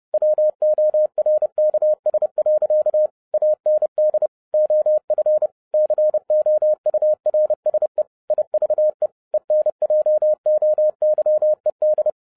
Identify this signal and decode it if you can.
Morse code, heard as a keyed tone, about 30 wpm, 610 Hz. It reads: WORKS. AND OF COURSE IVE ENJOYED
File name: 2012-01-mp3 / n11-I DIDNT MISS THE SOLAR CYCLE 24-30wpm